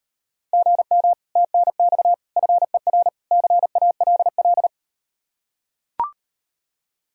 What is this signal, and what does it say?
Morse code, heard as a keyed tone, about 38 wpm, 700 Hz. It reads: GM TNX FER CALL